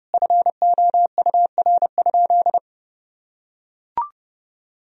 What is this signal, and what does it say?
Morse code, heard as a keyed tone, about 32 wpm, 700 Hz. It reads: FOUR?
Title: Sentences from Top 1000 Words - Review Part 2 - 30wpm